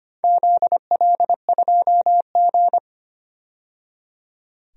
Morse code, heard as a keyed tone, about 25 wpm, 700 Hz. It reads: ZL2G
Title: DXCC-100 Call Signs - 4 Characters - 25wpm